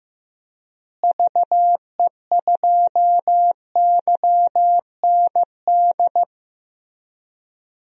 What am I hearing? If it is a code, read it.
Morse code, heard as a keyed tone, about 15 wpm, 700 Hz. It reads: VE2YND
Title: DXCC-100 Call Signs - 6 Characters - Repeated 3x - 15wpm